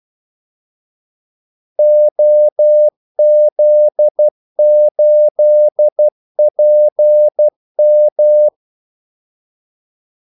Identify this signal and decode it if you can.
Morse code, heard as a keyed tone, about 12 wpm, 600 Hz. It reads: OZ8PM
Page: 2 ekstra ordmellemrum mellem kaldesignalerne